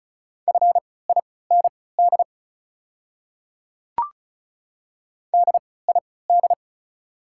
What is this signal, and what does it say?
Morse code, heard as a keyed tone, about 34 wpm, spaced as for 20 wpm, 700 Hz. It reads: FIND DID
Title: 3x Character Spacing - Top 100 Words - 35wpm